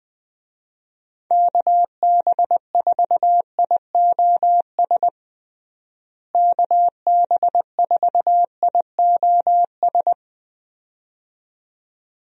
Morse code, heard as a keyed tone, about 20 wpm, 700 Hz. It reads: KB4IOS KB4IOS
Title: USA Call Signs - 6 Characters - Repeated 2x - 20wpm